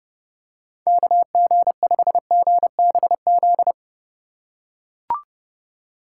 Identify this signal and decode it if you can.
Morse code, heard as a keyed tone, about 28 wpm, 700 Hz. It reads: KG5GBZ